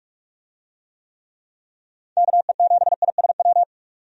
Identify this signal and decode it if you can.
Morse code, heard as a keyed tone, about 46 wpm, 700 Hz. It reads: KE7ISW
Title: USA Call Signs - 6 Characters - 45wpm